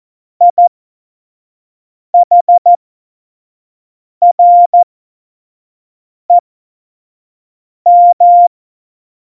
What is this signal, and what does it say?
Morse code, heard as a keyed tone, about 14 wpm, spaced as for 5 wpm, 700 Hz. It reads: IHREM
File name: Morsekurs / Witze / Witz 19. September 17, Tempo 25